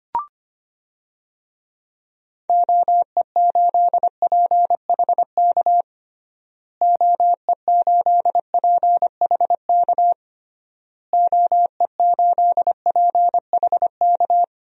Morse code, heard as a keyed tone, about 25 wpm, 700 Hz. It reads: OE8PHK OE8PHK OE8PHK
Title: DXCC-100 Call Signs - 6 Characters - Repeated 3x - 25wpm